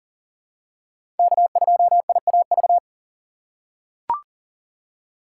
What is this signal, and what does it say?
Morse code, heard as a keyed tone, about 37 wpm, 700 Hz. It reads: K2IAV
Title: USA Call Signs - 5 Characters - 2x Character Spacing - 40wpm